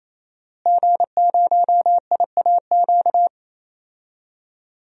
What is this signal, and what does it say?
Morse code, heard as a keyed tone, about 28 wpm, 700 Hz. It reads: G0IAQ
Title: DXCC-100 Call Signs - 5 Characters - Repeated 3x - 28wpm